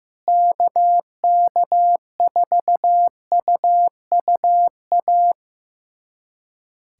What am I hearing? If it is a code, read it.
Morse code, heard as a keyed tone, about 15 wpm, 700 Hz. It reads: KK4UUA
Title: USA Callsign DE Callsign 15wpm